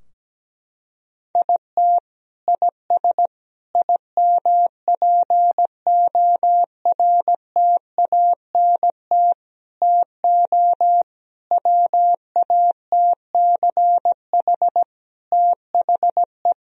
Morse code, heard as a keyed tone, about 17 wpm, 700 Hz. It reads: IT IS IMPORTANT TO WATCH THE
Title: Sentences from Top 1000 Words - Review Part 4 - 17wpm